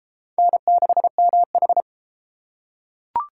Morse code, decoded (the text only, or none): N6MH